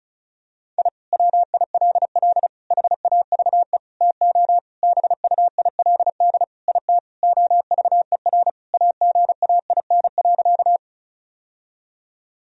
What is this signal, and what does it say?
Morse code, heard as a keyed tone, about 34 wpm, 700 Hz. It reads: I WILL HAVE TO BUILD IT OVER AGAIN.
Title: Sentences from Top 1000 Words - Review Part 1 - 35wpm